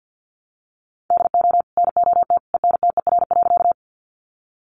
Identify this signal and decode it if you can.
Morse code, heard as a keyed tone, about 50 wpm, 700 Hz. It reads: DO NOT ENTER.
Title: Sentences from Top 1300 Words 50wpm